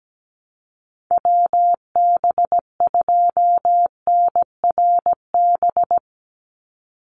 Morse code, heard as a keyed tone, about 17 wpm, 700 Hz. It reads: WB2NRB
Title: USA Call Signs 17wpm